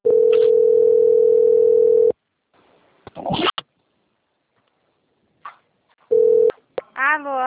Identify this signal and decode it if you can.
Ringback tone (North America), 440 + 480 Hz.